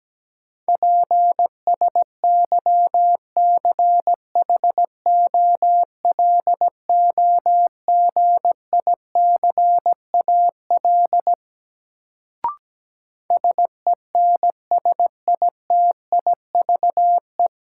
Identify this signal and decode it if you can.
Morse code, heard as a keyed tone, about 17 wpm, 700 Hz. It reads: PSYCHOLOGICAL SENSITIVE
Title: Sentences from Top 1400 Words - New Word Drill - 17wpm